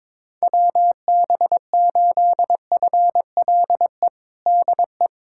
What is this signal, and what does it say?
Morse code, heard as a keyed tone, about 22 wpm, 700 Hz. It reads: WB8FLE DE
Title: USA Callsign DE Callsign - 6 Characters - 22wpm